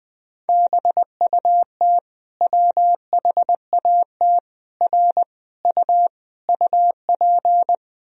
Morse code, heard as a keyed tone, about 19 wpm, 700 Hz. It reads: BUT WHAT R U UP
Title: Sentences from Top 100 Words and Abbreviations 20wpm